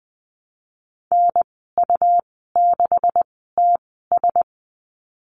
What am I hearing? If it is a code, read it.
Morse code, heard as a keyed tone, about 20 wpm, spaced as for 15 wpm, 700 Hz. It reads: NU6TS
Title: 128 Lesson 21g - Number 6 - Call Signs 20wpm